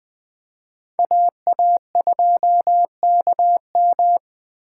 Morse code, heard as a keyed tone, about 20 wpm, 700 Hz. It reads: AA2KM